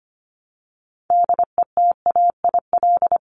Morse code, heard as a keyed tone, about 25 wpm, 700 Hz. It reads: DETAIL
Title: Top 1251-1500 Words 25wpm